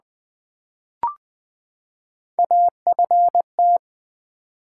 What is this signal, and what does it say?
Morse code, heard as a keyed tone, about 20 wpm, 700 Hz. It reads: AFT
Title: Build A Word - 3 Letters At A Time - 9-Letter Words - 20wpm